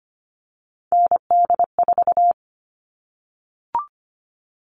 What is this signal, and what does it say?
Morse code, heard as a keyed tone, about 25 wpm, 700 Hz. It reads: ND4
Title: All Cut Numbers - 3-Digit 25wpm